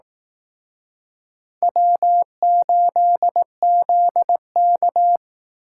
Morse code, heard as a keyed tone, about 18 wpm, 700 Hz. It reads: W8ZK
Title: Speed Racing - USA Call Signs 15wpm